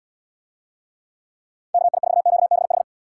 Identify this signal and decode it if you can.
Morse code, heard as a keyed tone, about 71 wpm, 700 Hz. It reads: KI4QBH